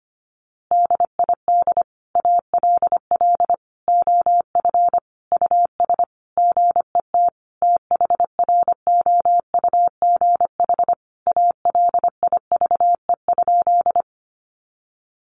Morse code, heard as a keyed tone, about 25 wpm, 700 Hz. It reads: DID ALL OF US GET THROUGH ALIVE?
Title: A Sound Of Thunder 25wpm